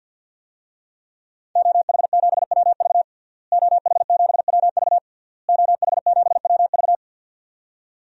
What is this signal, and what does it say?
Morse code, heard as a keyed tone, about 51 wpm, 700 Hz. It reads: OH7WV OH7WV OH7WV
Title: DXCC-100 Call Signs - 5 Characters - Repeated 3x - 50wpm